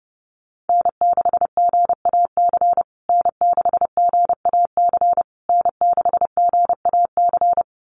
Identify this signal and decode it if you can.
Morse code, heard as a keyed tone, about 30 wpm, 700 Hz. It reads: N6GAC N6GAC N6GAC
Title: USA Call Signs Repeated 3x 30wpm